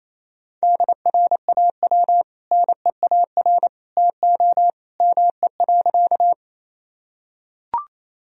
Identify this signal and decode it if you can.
Morse code, heard as a keyed tone, about 28 wpm, 700 Hz. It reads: DRAW NEAR TO ME.
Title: Sentences from Top 1000 Words - Review Part 1 - 28wpm